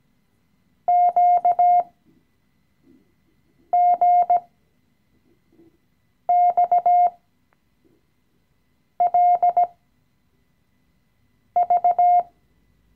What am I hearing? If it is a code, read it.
Morse code, heard as a keyed tone, about 17 wpm, spaced as for 4 wpm, 710 Hz. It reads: QGXLV
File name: CW-Online-Kurs / Anfänger / Lektion 016